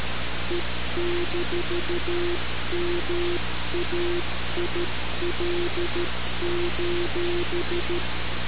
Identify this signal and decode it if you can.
Morse code, heard as a keyed tone, about 13 wpm, 350 Hz, with noise. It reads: E-MAIL:
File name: radiocap / REC z